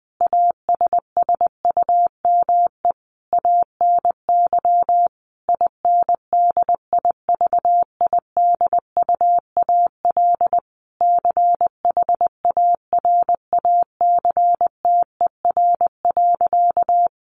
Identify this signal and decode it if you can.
Morse code, heard as a keyed tone, about 20 wpm, 700 Hz. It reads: ASSUME ANY INDIVIDUAL CHARACTER.
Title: Sentences from Top 1500 Words 20wpm